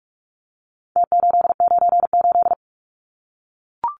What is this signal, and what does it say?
Morse code, heard as a keyed tone, about 44 wpm, 700 Hz. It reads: T898
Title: Numbers - Contest Style - 4-Digit 45wpm